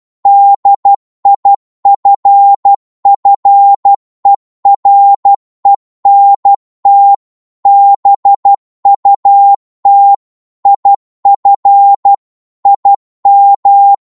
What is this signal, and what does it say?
Morse code, heard as a keyed tone, about 12 wpm, 800 Hz. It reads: DIFFERENT BUT IF IM